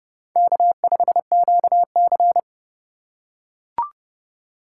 Morse code, heard as a keyed tone, about 30 wpm, 700 Hz. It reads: K5QC